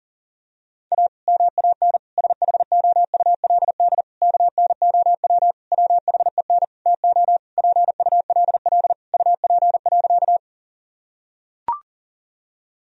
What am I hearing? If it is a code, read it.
Morse code, heard as a keyed tone, about 40 wpm, 700 Hz. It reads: A MAN SHOULD KNOW WHEN TO PULL UP.
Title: Sentences from Top 1000 Words - Review Part 2 - 40wpm